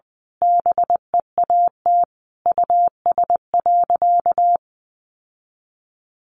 Morse code, heard as a keyed tone, about 20 wpm, 700 Hz. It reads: BEAT US.
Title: Sentences from Top 1300 Words 20wpm